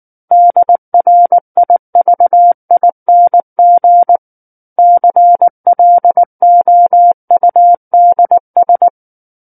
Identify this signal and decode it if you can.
Morse code, heard as a keyed tone, about 19 wpm, 700 Hz. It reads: DRIVING CLOUDS
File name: Book One / 10-Chapter 10 - 19 WPM